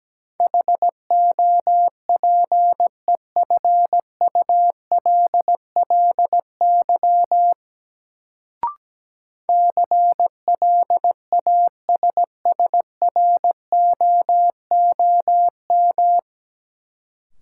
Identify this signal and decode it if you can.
Morse code, heard as a keyed tone, about 17 wpm, 700 Hz. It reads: HOPEFULLY CLASSROOM
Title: Sentences from Top 1500 Words - New Word Drill - 17wpm